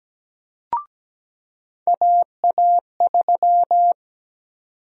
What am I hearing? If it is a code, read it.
Morse code, heard as a keyed tone, about 17 wpm, 700 Hz. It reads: AA3
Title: USA Call Sign Prefixes - 17wpm